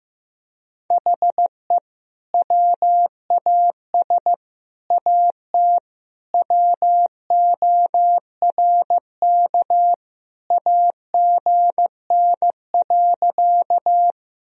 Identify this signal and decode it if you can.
Morse code, heard as a keyed tone, about 15 wpm, 700 Hz. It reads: HE WAS AT WORK AGN.
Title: Sentences from Top 200 Words and Abbreviations 15wpm